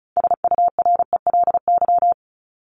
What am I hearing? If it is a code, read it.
Morse code, heard as a keyed tone, about 34 wpm, 700 Hz. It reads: SURELY
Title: LY Suffix - Top 100 Words with Base Words 35wpm